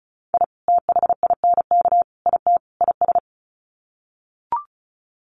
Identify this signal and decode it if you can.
Morse code, heard as a keyed tone, about 34 wpm, 700 Hz. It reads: I THINK IT IS
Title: Sets of 4 Words 35wpm